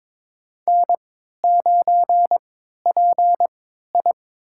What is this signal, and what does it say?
Morse code, heard as a keyed tone, about 22 wpm, spaced as for 13 wpm, 700 Hz. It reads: N9PI
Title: USA Call Signs - 4 Characters - 3x Character Spacing - 22wpm